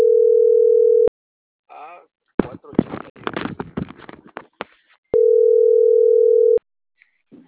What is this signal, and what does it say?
Ringback tone (North America), 440 + 480 Hz, with speech.
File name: CENTRAL MAERSK